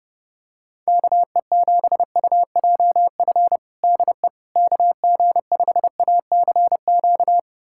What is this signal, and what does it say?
Morse code, heard as a keyed tone, about 30 wpm, 700 Hz. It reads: KE7UJF DE KG5ACQ
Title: USA Callsign DE Callsign - 6 Characters - 30wpm